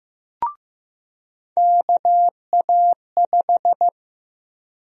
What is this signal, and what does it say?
Morse code, heard as a keyed tone, about 15 wpm, 700 Hz. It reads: KA5
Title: USA Call Sign Prefixes - Repeated 2x - 15wpm